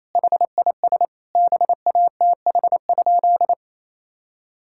Morse code, heard as a keyed tone, about 28 wpm, 700 Hz. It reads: HIS BATH?
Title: Sentences from Top 1600 Words 28wpm